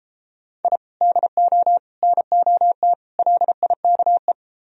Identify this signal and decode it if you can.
Morse code, heard as a keyed tone, about 34 wpm, 700 Hz. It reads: I DO NOT LIKE